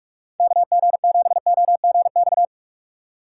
Morse code, heard as a keyed tone, about 45 wpm, 700 Hz. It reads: KG7OGX